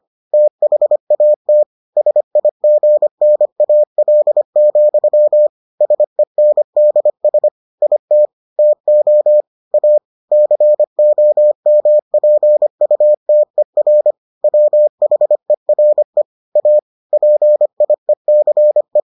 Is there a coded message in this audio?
Yes: Morse code from 0.3 to 19.0 s, heard as a keyed tone, about 25 wpm, 600 Hz. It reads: THAT SIGNAL, SENDS IT TO A COMPUTER WHERE A PIECE